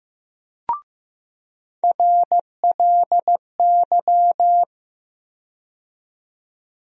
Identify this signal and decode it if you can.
Morse code, heard as a keyed tone, about 15 wpm, 700 Hz. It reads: RLY
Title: USA Call Sign Postfixes - 3 Letters 15wpm